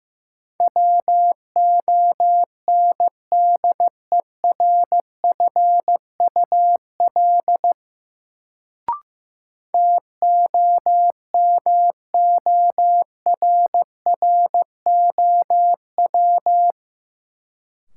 Morse code, heard as a keyed tone, about 15 wpm, 700 Hz. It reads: WONDERFUL TOMORROW
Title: Sentences from Top 1100 Words - New Word Drill Alt - 15wpm